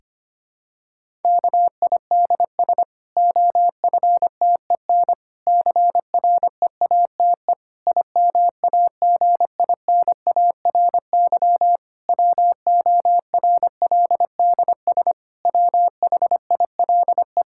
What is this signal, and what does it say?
Morse code, heard as a keyed tone, about 25 wpm, 700 Hz. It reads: KIDS OFTEN CREATE IMAGINARY WORLDS WHILE
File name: practice / 25wpm / Difficult Sentences with Hints - 25wpm